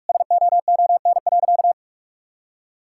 Morse code, heard as a keyed tone, about 42 wpm, 700 Hz. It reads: SOON.
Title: Sentences from Top 1000 Words - Review Part 6 - 45wpm